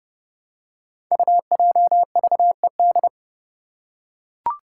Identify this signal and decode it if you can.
Morse code, heard as a keyed tone, about 30 wpm, 700 Hz. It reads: UJVED
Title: Five Challenging Letter-Number Combinations 30wpm